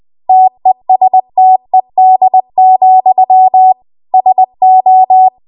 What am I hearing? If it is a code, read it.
Morse code, heard as a keyed tone, about 20 wpm, 750 Hz. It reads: TESTED, SO
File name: Archive / 20 WPM / 20WPM